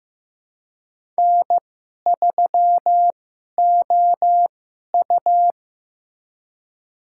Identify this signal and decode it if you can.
Morse code, heard as a keyed tone, about 16 wpm, spaced as for 11 wpm, 700 Hz. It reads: N3OU
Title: USA Call Signs - 2x Character Spacing - 15wpm